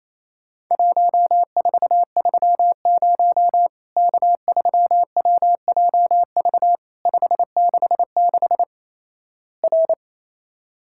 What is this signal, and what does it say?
Morse code, heard as a keyed tone, about 28 wpm, 700 Hz. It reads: 1430 K3WJV 566 R
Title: Worked All Europe DX Contest - QTC Exchange - 28wpm